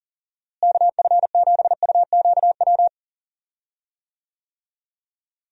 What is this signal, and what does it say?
Morse code, heard as a keyed tone, about 40 wpm, 700 Hz. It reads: KF7UQW
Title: USA Call Signs - 6 Characters - Repeated 2x - 40wpm